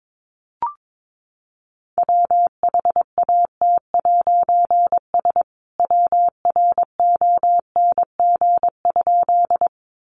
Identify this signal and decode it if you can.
Morse code, heard as a keyed tone, about 22 wpm, 700 Hz. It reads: WHAT'S WRONG?
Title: A Sound Of Thunder 22wpm